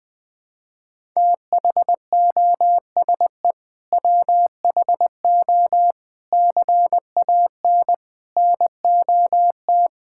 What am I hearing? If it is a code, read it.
Morse code, heard as a keyed tone, about 20 wpm, 700 Hz. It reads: THOSE WHO CAN NOT
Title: Sentences with 8-Word Phrases - 20wpm